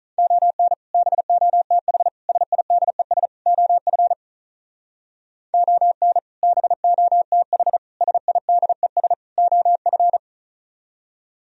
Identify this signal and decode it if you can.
Morse code, heard as a keyed tone, about 46 wpm, 700 Hz. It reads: ON BOTH SIDES OF ON BOTH SIDES OF